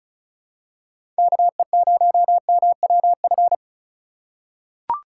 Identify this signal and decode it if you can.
Morse code, heard as a keyed tone, about 36 wpm, 700 Hz. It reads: KE0MWF
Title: USA Call Signs - 3x Character Spacing - 35wpm